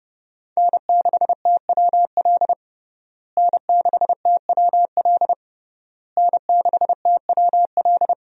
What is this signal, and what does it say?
Morse code, heard as a keyed tone, about 30 wpm, 700 Hz. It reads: N6TWL N6TWL N6TWL
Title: USA Call Signs - 5 Characters - Repeated 3x - 30wpm